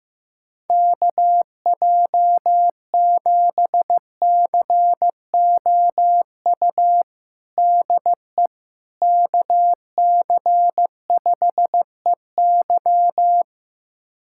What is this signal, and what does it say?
Morse code, heard as a keyed tone, about 15 wpm, 700 Hz. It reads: KJ7COU DE KC5EY
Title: USA Callsign DE Callsign 15wpm